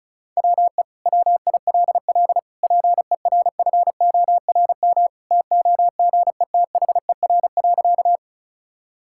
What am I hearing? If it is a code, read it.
Morse code, heard as a keyed tone, about 34 wpm, 700 Hz. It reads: WE WILL PERFORM TOGETHER.
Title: Sentences from Top 1000 Words - Review Part 5 - 35wpm